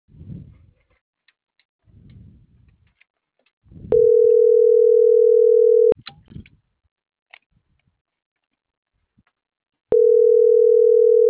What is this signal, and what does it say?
Ringback tone (North America), 440 + 480 Hz.
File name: CENTRAL MAERSK / out gabrielle.melendez - .25778